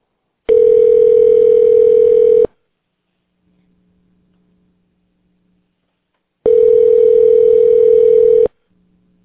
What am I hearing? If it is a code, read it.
Ringback tone (North America), 440 + 480 Hz.